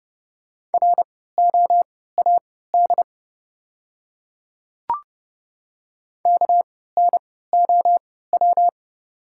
Morse code, heard as a keyed tone, about 30 wpm, spaced as for 17 wpm, 690 Hz. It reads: ROAD KNOW
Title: 3x Character Spacing - Top 500 Words - 30wpm